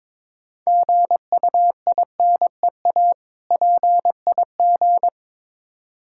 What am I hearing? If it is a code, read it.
Morse code, heard as a keyed tone, about 22 wpm, 700 Hz. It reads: GUINEA PIG
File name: Difficult Words with Hints - 22wpm